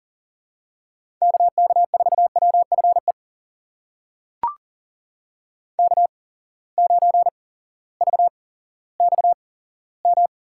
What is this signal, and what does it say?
Morse code, heard as a keyed tone, about 39 wpm, spaced as for 18 wpm, 700 Hz. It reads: KK4WFE K9VXM
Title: USA Call Signs - 8x Character Spacing - 40wpm